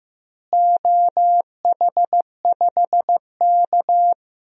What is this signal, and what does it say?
Morse code, heard as a keyed tone, about 15 wpm, 700 Hz. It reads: OH5K
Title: DXCC-100 Call Signs - 4 Characters - Repeated 2x - 15wpm